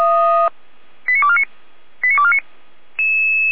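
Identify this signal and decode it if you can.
Five-tone selective call (ZVEI), heard as quick runs of five tones, tones of about 70 ms. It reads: 89249 89249
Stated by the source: Five-tone ZVEI selective call received with an SDR.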